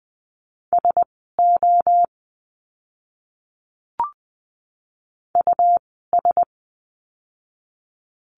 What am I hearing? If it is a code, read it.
Morse code, heard as a keyed tone, about 20 wpm, spaced as for 15 wpm, 700 Hz. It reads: SO US
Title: 230 Lesson 37f - Letter x - Words 20wpm